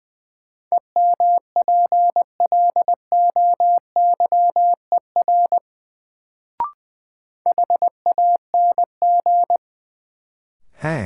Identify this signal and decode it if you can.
Morse code, heard as a keyed tone, about 21 wpm, 700 Hz. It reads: EMPLOYER HANG
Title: Sentences from Top 1600 Words - New Word Drill - 20wpm